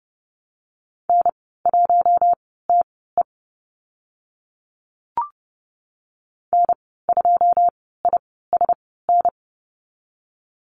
Morse code, heard as a keyed tone, about 30 wpm, spaced as for 17 wpm, 700 Hz. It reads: N1TE N2ISN
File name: Lesson 14g - Number 2 - Call Signs 30wpm